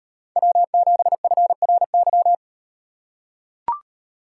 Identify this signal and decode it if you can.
Morse code, heard as a keyed tone, about 36 wpm, 700 Hz. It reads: W7FRY